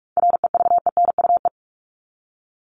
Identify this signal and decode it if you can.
Morse code, heard as a keyed tone, about 44 wpm, 700 Hz. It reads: REVENUE
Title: Top 1501-1750 Words 45wpm